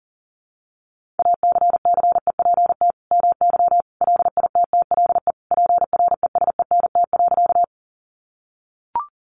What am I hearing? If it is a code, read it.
Morse code, heard as a keyed tone, about 40 wpm, 700 Hz. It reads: ACCEPT MY LITTLE PRESENT.
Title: Sentences from Top 1300 Words 40wpm